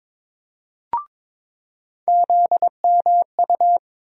Morse code, heard as a keyed tone, about 22 wpm, 700 Hz. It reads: ZMU
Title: USA Call Sign Postfixes - 3 Letters 22wpm